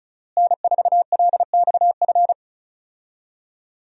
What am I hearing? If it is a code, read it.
Morse code, heard as a keyed tone, about 34 wpm, 700 Hz. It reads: N4LXF
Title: USA Call Signs - 5 Characters - Repeated 2x - 35wpm